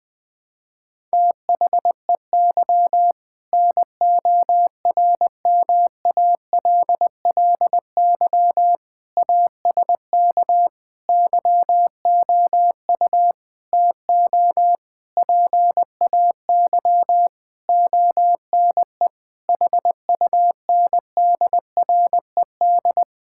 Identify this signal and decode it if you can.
Morse code, heard as a keyed tone, about 21 wpm, 700 Hz. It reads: THEY NORMALLY ASK YOU TO PAY ONE HUNDRED
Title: Sentences from Top 1000 Words - Review Part 6 - 20wpm